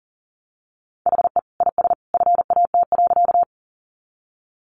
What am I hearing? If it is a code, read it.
Morse code, heard as a keyed tone, about 40 wpm, 700 Hz. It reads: HE IS FAT.
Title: Easy Phrases 40wpm